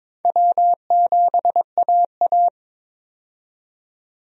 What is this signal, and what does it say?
Morse code, heard as a keyed tone, about 22 wpm, 700 Hz. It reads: W7AA
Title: USA Call Signs - 4 Characters - 22wpm